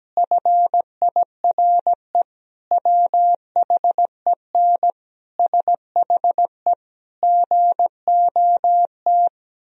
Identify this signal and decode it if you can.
Morse code, heard as a keyed tone, about 17 wpm, 700 Hz. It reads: FIRE WHEN SHE GOT
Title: Sentences from Top 1000 Words - Review Part 1 - 17wpm